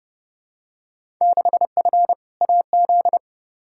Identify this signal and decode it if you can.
Morse code, heard as a keyed tone, about 30 wpm, 700 Hz. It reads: 6F AZ
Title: Field Day Exchanges Encore - 30wpm